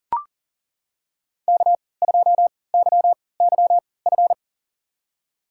Morse code, heard as a keyed tone, about 40 wpm, spaced as for 23 wpm, 690 Hz. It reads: K2YYF
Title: USA Call Signs - 5 Characters - 3x Character Spacing - 40wpm